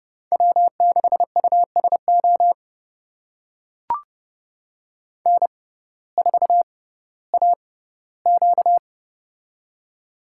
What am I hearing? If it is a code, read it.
Morse code, heard as a keyed tone, about 30 wpm, spaced as for 17 wpm, 700 Hz. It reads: W6USO N4AQ
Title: USA Call Signs - 6x Character Spacing - 30wpm